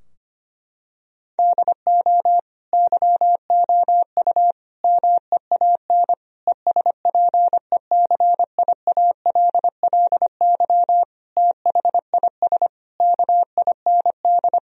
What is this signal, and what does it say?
Morse code, heard as a keyed tone, about 25 wpm, 700 Hz. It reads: DO YOU MEAN ESPECIALLY THIS KIND
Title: Sentences from Top 1000 Words - Review Part 4 - 25wpm